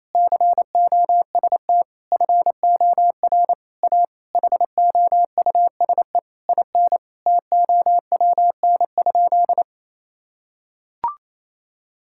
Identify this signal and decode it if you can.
Morse code, heard as a keyed tone, about 28 wpm, 700 Hz. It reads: COST FOR A HOUSE IN TOWN?
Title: Sentences from Top 1000 Words - Review Part 6 - 28wpm